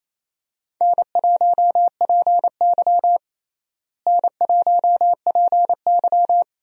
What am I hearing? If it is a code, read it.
Morse code, heard as a keyed tone, about 28 wpm, 700 Hz. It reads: N1PY N1PY
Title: USA Call Signs - 4 Characters - Repeated 2x - 28wpm